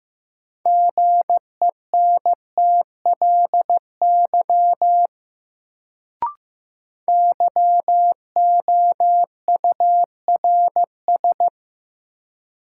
Morse code, heard as a keyed tone, about 15 wpm, 700 Hz. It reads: GENTLY YOURS
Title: Sentences from Top 1500 Words - New Word Drill Alt - 15wpm